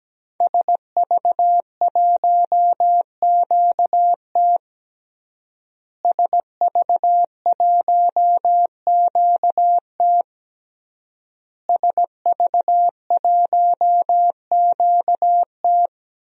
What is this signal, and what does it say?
Morse code, heard as a keyed tone, about 17 wpm, 700 Hz. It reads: SV1QT SV1QT SV1QT
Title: DXCC-100 Call Signs - 5 Characters - Repeated 3x - 17wpm